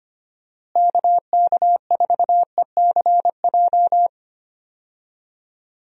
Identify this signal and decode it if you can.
Morse code, heard as a keyed tone, about 25 wpm, 700 Hz. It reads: KK4ECJ
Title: USA Call Signs - 6 Characters - Repeated 3x - 25wpm